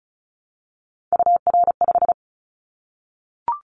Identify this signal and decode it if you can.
Morse code, heard as a keyed tone, about 36 wpm, 700 Hz. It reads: UR5
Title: Callsign Prefixes for Top 25 DXCC - 35wpm